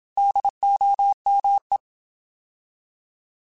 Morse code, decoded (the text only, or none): DOME